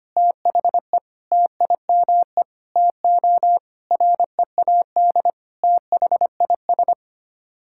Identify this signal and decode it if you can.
Morse code, heard as a keyed tone, about 25 wpm, 700 Hz. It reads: THE TIME TO READ THIS
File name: Sets of 9 Words - 25wpm